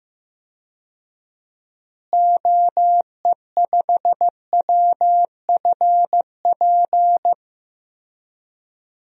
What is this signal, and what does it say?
Morse code, heard as a keyed tone, about 15 wpm, 700 Hz. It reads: OE5WFP
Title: DXCC-100 Call Signs - 6 Characters - Repeated 2x - 15wpm